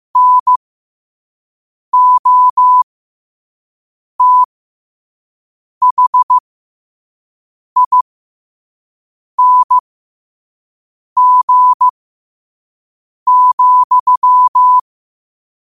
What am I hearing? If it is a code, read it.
Morse code, heard as a keyed tone, about 15 wpm, spaced as for 5 wpm, 1000 Hz. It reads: NOTHING,
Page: New quotes every day in morse code at 5 Words per minute.